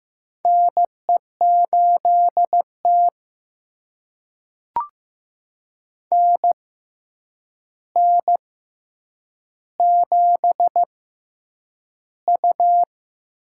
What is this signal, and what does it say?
Morse code, heard as a keyed tone, about 15 wpm, spaced as for 8 wpm, 700 Hz. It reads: NE8T NN7U
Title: USA Call Signs - 4 Characters - 6x Character Spacing - 15wpm